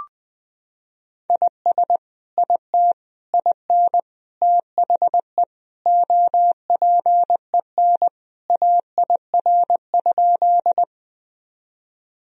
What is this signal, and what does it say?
Morse code, heard as a keyed tone, about 21 wpm, 700 Hz. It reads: IS IT IN THE OPEN AIR?
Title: Sentences from Top 1000 Words - Review Part 1 - 20wpm